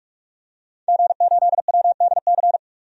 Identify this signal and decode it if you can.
Morse code, heard as a keyed tone, about 46 wpm, 700 Hz. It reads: G8WDC
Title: DXCC-100 Call Signs - 5 Characters - 45wpm